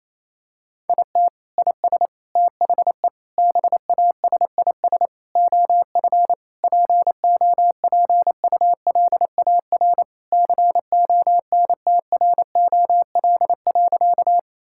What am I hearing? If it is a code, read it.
Morse code, heard as a keyed tone, about 28 wpm, 700 Hz. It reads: IT IS THE BASIS OF POPULAR CONTROL.
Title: Sentences from Top 1000 Words - Review Part 5 - 28wpm